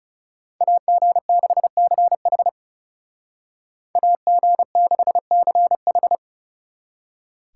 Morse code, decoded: AG6CH AG6CH

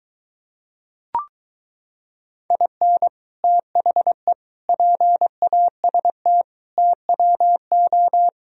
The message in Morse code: IN THE PAST TWO

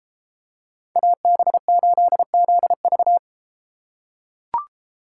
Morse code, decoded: AB8ZV

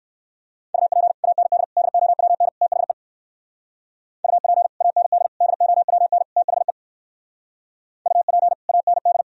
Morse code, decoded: UP AND DOWN THE UP AND DOWN THE UP AND

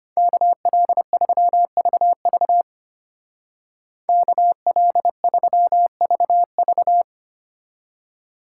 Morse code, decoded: KL3VV KL3VV

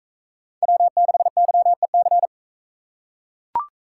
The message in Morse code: W6YEC

